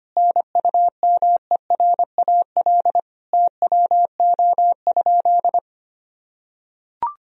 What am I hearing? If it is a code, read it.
Morse code, heard as a keyed tone, about 25 wpm, 700 Hz. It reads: NUMERAL TWO?
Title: Sentences from Top 1000 Words - Review Part 2 - 25wpm